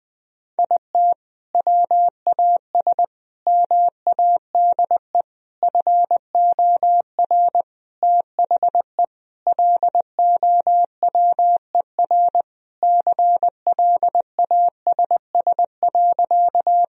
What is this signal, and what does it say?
Morse code, heard as a keyed tone, about 20 wpm, 700 Hz. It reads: IT WAS MADE FOR THE LOWER CLASS.
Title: Sentences from Top 1000 Words - Review Part 4 - 20wpm